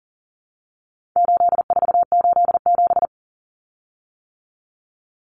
Morse code, heard as a keyed tone, about 37 wpm, 700 Hz. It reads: 8487
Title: Common Cut Numbers - 4-Digit 40wpm